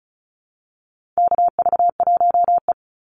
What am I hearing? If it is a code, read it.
Morse code, heard as a keyed tone, about 34 wpm, 700 Hz. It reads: KV1E
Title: Call Signs - 35wpm